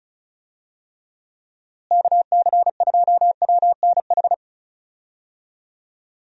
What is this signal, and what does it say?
Morse code, heard as a keyed tone, about 36 wpm, 700 Hz. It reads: KC2WNH
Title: USA Call Signs - 6 Characters - Repeated 3x - 35wpm